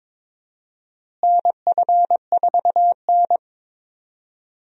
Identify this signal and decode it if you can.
Morse code, heard as a keyed tone, about 22 wpm, 700 Hz. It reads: NF4N
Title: USA Call Signs - 4 Characters - Repeated 2x - 22wpm